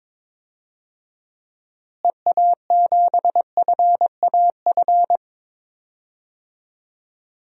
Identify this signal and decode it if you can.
Morse code, heard as a keyed tone, about 21 wpm, 700 Hz. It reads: EA7FAF
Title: DXCC-100 Call Signs - 6 Characters - Repeated 3x - 22wpm